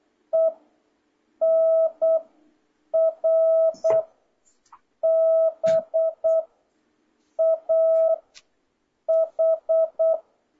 Morse code, heard as a keyed tone, about 8 wpm, 650 Hz. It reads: E N R B A H